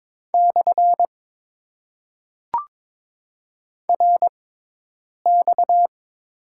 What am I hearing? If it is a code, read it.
Morse code, heard as a keyed tone, about 22 wpm, spaced as for 8 wpm, 700 Hz. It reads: / RX
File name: Top 90 QSO Elements - 6x Character Spacing - 22wpm